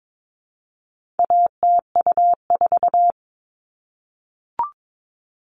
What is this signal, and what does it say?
Morse code, heard as a keyed tone, about 22 wpm, 700 Hz. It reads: ATU4
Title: All Cut Numbers - 4-Digit 22wpm